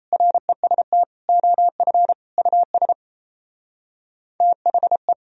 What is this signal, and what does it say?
Morse code, heard as a keyed tone, about 34 wpm, 700 Hz. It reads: REST OF US THE
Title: Speed-Racing - Sets of 4 Words 28wpm